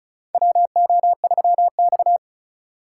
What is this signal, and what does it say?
Morse code, heard as a keyed tone, about 36 wpm, 700 Hz. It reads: WO3X
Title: USA Call Signs - 4 Characters - 3x Character Spacing - 35wpm